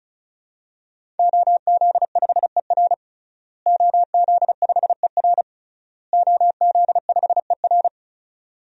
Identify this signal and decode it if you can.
Morse code, heard as a keyed tone, about 36 wpm, 700 Hz. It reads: OZ5ER OZ5ER OZ5ER
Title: DXCC-100 Call Signs - 5 Characters - Repeated 3x - 35wpm